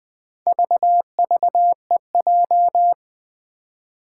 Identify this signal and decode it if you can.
Morse code, heard as a keyed tone, about 20 wpm, 700 Hz. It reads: VVEJ
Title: Four Challenging Letter-Number Combinations 20wpm